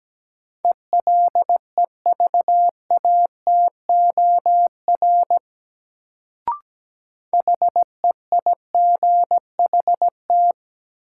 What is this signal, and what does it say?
Morse code, heard as a keyed tone, about 17 wpm, 700 Hz. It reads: ELEVATOR HEIGHT
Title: Sentences from Top 1600 Words - New Word Drill Alt - 17wpm